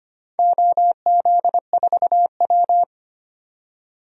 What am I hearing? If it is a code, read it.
Morse code, heard as a keyed tone, about 25 wpm, 700 Hz. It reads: OZ4W
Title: DXCC-100 Call Signs - 4 Characters - 25wpm